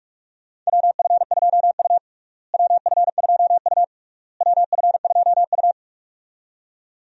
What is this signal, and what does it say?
Morse code, heard as a keyed tone, about 46 wpm, 700 Hz. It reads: WF2U WF2U WF2U
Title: USA Call Signs - 4 Characters - Repeated 3x - 45wpm